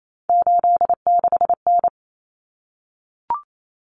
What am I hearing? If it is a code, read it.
Morse code, heard as a keyed tone, about 28 wpm, 700 Hz. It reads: 86N
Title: Common Cut Numbers - 3-Digit 28wpm